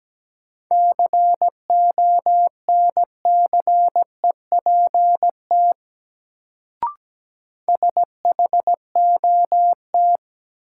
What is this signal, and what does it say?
Morse code, heard as a keyed tone, about 17 wpm, 700 Hz. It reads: CONCEPT SHOT
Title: Sentences from Top 1100 Words - New Word Drill - 17wpm